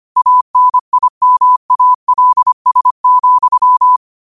Morse code, heard as a keyed tone, about 25 wpm, 1000 Hz. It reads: ANIMALS,